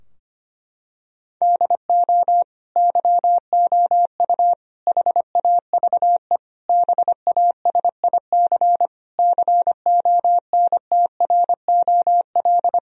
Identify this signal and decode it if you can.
Morse code, heard as a keyed tone, about 25 wpm, 700 Hz. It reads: DO YOU HAVE BASIC CONTROL
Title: Sentences from Top 1000 Words - Review Part 3 - 25wpm